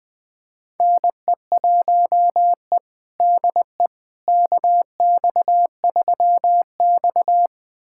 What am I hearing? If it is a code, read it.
Morse code, heard as a keyed tone, about 20 wpm, 700 Hz. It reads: NE1E DE KX3X
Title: USA Callsign DE Callsign - 4 Characters - 20wpm